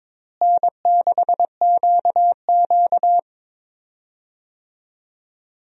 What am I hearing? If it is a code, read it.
Morse code, heard as a keyed tone, about 22 wpm, 700 Hz. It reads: N6QQ
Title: USA Call Signs - 4 Characters - Repeated 2x - 22wpm